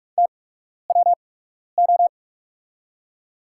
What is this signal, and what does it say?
Morse code, heard as a keyed tone, about 48 wpm, spaced as for 12 wpm, 700 Hz. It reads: TWO
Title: 8x Character Spacing - Top 100 Words - 45wpm